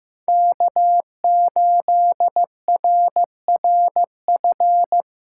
Morse code, heard as a keyed tone, about 15 wpm, 700 Hz. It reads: K8RRF
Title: USA Call Signs - 3x Character Spacing - 15wpm